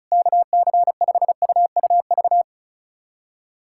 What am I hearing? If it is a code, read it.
Morse code, heard as a keyed tone, about 34 wpm, 700 Hz. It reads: KC5UUV
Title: USA Callsign DE Callsign - 6 Characters - 35wpm